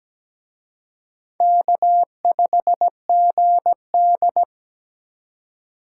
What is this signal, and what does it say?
Morse code, heard as a keyed tone, about 20 wpm, 700 Hz. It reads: K5GD
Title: Speed Racing - USA 1x2 Call Signs 17wpm